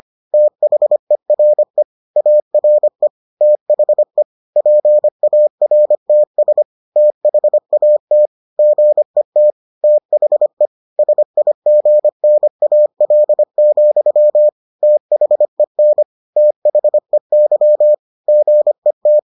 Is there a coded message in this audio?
Yes: Morse code from 0.3 to 19.2 s, heard as a keyed tone, about 25 wpm, 600 Hz. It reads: THERE ARE THE PARTS THAT GET THE SIGNAL, THEN THEY GET